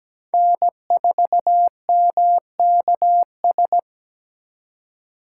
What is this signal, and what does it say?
Morse code, heard as a keyed tone, about 20 wpm, 700 Hz. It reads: N4MKS
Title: Speed Racing - USA 1x3 Call Signs 17wpm